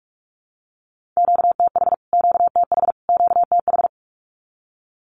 Morse code, heard as a keyed tone, about 44 wpm, 700 Hz. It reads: QTH QTH QTH
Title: Q-Codes Repeated 3x 45wpm